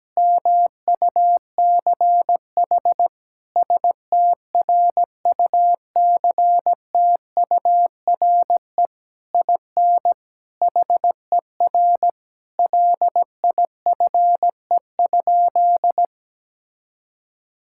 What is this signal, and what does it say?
Morse code, heard as a keyed tone, about 17 wpm, 700 Hz. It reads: MUCH STRUCTURE IN HER LIFE?
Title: Sentences from Top 1000 Words - Review Part 6 - 17wpm